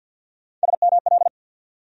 Morse code, heard as a keyed tone, about 51 wpm, 700 Hz. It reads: SML
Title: USA Call Sign Postfixes - 3 Letters 50wpm